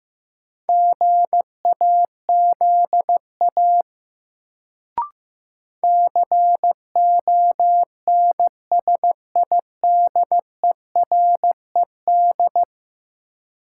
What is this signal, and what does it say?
Morse code, heard as a keyed tone, about 15 wpm, 700 Hz. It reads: GAZA CONSIDERED
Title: Sets of 5 Words - New Word Drill - 15wpm